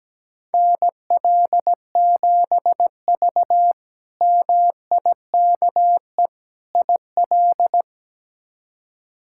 Morse code, heard as a keyed tone, about 17 wpm, 700 Hz. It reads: NL7V MIKE IL
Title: K1USN SST QSO Practice - 17wpm